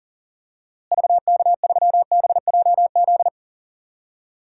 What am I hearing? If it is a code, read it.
Morse code, heard as a keyed tone, about 37 wpm, 700 Hz. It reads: VK3BJZ
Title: DXCC-100 Call Signs - 6 Characters - Repeated 3x - 40wpm